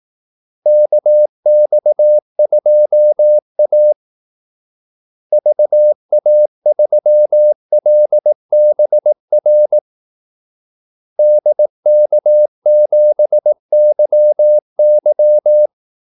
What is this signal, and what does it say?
Morse code, heard as a keyed tone, about 18 wpm, 600 Hz. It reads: KX2A VA3LBR DK7YY
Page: Kaldesignaler 18 wpm